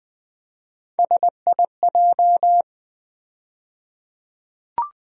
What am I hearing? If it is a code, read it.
Morse code, heard as a keyed tone, about 20 wpm, 700 Hz. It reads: SIJ
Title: USA Call Sign Postfixes - 3 Letters 20wpm